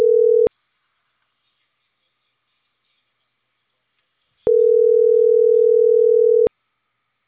Ringback tone (North America), 440 + 480 Hz.